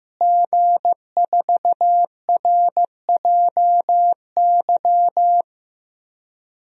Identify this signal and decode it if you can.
Morse code, heard as a keyed tone, about 15 wpm, 700 Hz. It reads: G4RJY
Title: DXCC-100 Call Signs - 5 Characters - Repeated 3x - 15wpm